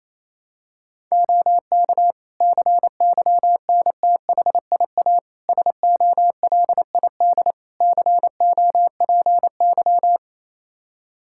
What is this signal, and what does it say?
Morse code, heard as a keyed tone, about 28 wpm, 700 Hz. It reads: OK CYNTHIA SOLID COPY
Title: Practice QSO - Exchange 3 - More Info - 28wpm